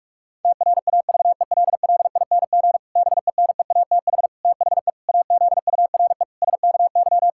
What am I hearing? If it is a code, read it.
Morse code, heard as a keyed tone, about 46 wpm, 700 Hz. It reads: TRAVELLING BENEATH THE AZURE SKY
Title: Pangram Challenge v2 - 45wpm